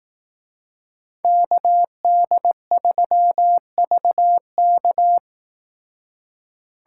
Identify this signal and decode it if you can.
Morse code, heard as a keyed tone, about 19 wpm, 700 Hz. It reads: KD3VK
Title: Speed Racing - USA 2x2 Call Signs 15wpm